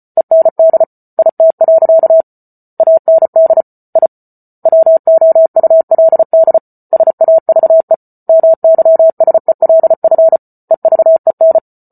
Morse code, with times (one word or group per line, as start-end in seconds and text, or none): 0.2-0.8 END
1.2-2.2 IT.
2.8-3.6 AND
4.0-4.1 I
4.7-6.6 WOULD
6.9-7.9 SAVE
8.3-10.4 MYSELF
10.7-11.6 EVEN